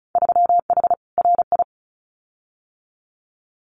Morse code, heard as a keyed tone, about 34 wpm, 700 Hz. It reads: 3H RI
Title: Winter Field Day Exchanges Encore 35wpm